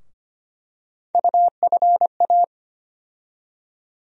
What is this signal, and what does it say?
Morse code, heard as a keyed tone, about 25 wpm, 700 Hz. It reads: UFA
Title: USA Call Sign Postfixes - 3 Letters 25wpm